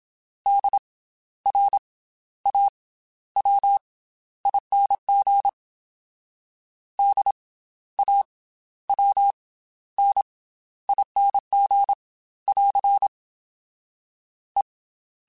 Morse code, heard as a keyed tone, about 27 wpm, spaced as for 13 wpm, 800 Hz. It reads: DRAWING DAWNING <AR> E
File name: Special INGs / Special INGs-mp3s / Slow ing @25